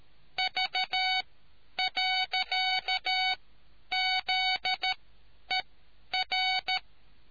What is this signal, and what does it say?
Morse code, heard as a keyed tone, about 13 wpm, spaced as for 9 wpm, 750 Hz. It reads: V.ZER